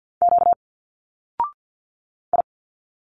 Morse code, heard as a keyed tone, about 50 wpm, 700 Hz. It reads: Q I
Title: Single Letters 50wpm